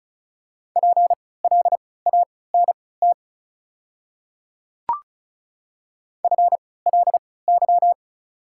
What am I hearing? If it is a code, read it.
Morse code, heard as a keyed tone, about 36 wpm, spaced as for 20 wpm, 690 Hz. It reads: PLANT FLY
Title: 3x Character Spacing - Top 500 Words - 35wpm